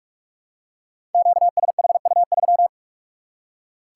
Morse code, heard as a keyed tone, about 44 wpm, 700 Hz. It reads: QSHU3
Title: Code Groups - 5 Letter:Numbers - 45wpm